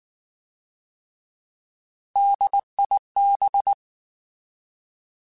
Morse code, heard as a keyed tone, about 19 wpm, 800 Hz. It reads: DIB
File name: Alphabetical / Alphabetical mp3s / Ds @18